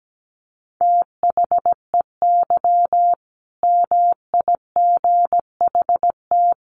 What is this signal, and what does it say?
Morse code, heard as a keyed tone, about 17 wpm, 700 Hz. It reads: THEY MIGHT
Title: Sentences from Top 1100 Words 17wpm